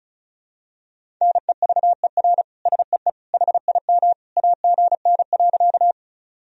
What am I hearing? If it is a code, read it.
Morse code, heard as a keyed tone, about 36 wpm, 700 Hz. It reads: NEVER SEE HIM AGN.
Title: Sentences from Top 300 Words and Abbreviations 35wpm